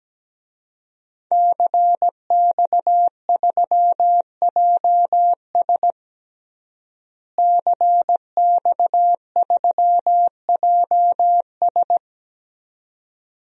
Morse code, heard as a keyed tone, about 17 wpm, 700 Hz. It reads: CX3JS CX3JS
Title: DXCC-100 Call Signs - 5 Characters - Repeated 2x - 17wpm